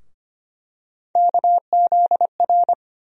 Morse code, heard as a keyed tone, about 25 wpm, 700 Hz. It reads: KZR
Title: USA Call Sign Postfixes - 3 Letters 25wpm